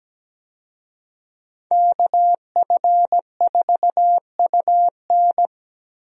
Morse code, heard as a keyed tone, about 17 wpm, 700 Hz. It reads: KF4UN
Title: USA Call Signs - 5 Characters - Repeated 3x - 17wpm